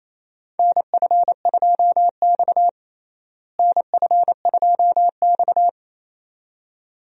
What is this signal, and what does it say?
Morse code, heard as a keyed tone, about 28 wpm, 700 Hz. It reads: NF2X NF2X
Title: USA Call Signs - 4 Characters - Repeated 2x - 28wpm